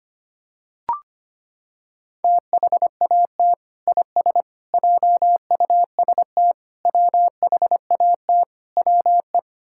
Morse code, heard as a keyed tone, about 25 wpm, 700 Hz. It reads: THAT IS JUST WHAT WE
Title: Sentences from Top 1600 Words 25wpm